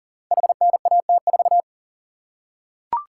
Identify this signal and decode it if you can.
Morse code, heard as a keyed tone, about 40 wpm, 700 Hz. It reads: HNAT4
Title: Code Groups - 5 Letter:Numbers - 40wpm